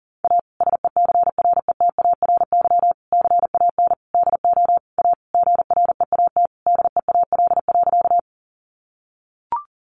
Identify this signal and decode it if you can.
Morse code, heard as a keyed tone, about 40 wpm, 700 Hz. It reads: A SECRETARY CAN DO A GREAT DEAL.
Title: Sentences from Top 1300 Words 40wpm